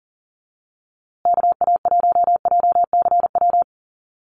Morse code, heard as a keyed tone, about 40 wpm, 700 Hz. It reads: KA1JCW
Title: USA Call Signs Repeated 2x 40wpm